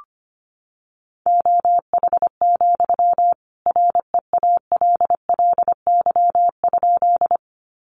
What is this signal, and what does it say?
Morse code, heard as a keyed tone, about 25 wpm, 700 Hz. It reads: OH, REALLY?
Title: Sentences from Top 600 Words 25wpm